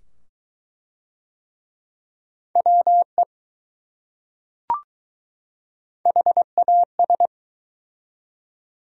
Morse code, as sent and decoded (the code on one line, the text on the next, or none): .-- . / .... .- ...
WE HAS